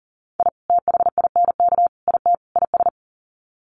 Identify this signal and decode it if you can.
Morse code, heard as a keyed tone, about 40 wpm, 700 Hz. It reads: I THINK IT IS
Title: Sets of 4 Words 40wpm